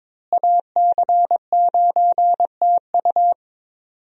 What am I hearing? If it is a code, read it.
Morse code, heard as a keyed tone, about 22 wpm, 700 Hz. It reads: AC9TU
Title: USA Call Signs - 5 Characters - Repeated 2x - 22wpm